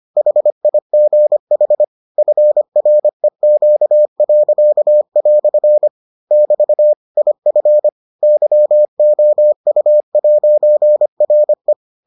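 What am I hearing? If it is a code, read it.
Morse code, heard as a keyed tone, about 25 wpm, 600 Hz. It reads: HIGH FREQ." <BT> IF YOU'RE
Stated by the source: Morse edition of the Foundations of Amateur Radio episodes.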